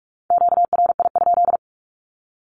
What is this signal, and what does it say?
Morse code, heard as a keyed tone, about 45 wpm, 700 Hz. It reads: QRI?
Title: Q-Codes 45wpm